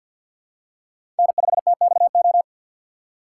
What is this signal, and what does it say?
Morse code, heard as a keyed tone, about 52 wpm, 700 Hz. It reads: N5TXO
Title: USA Call Signs - 5 Characters - 2x Character Spacing - 50wpm